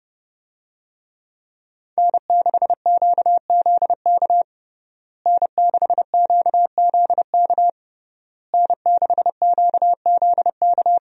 Morse code, heard as a keyed tone, about 30 wpm, 700 Hz. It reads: N6QZK N6QZK N6QZK
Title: USA Call Signs - 5 Characters - Repeated 3x - 30wpm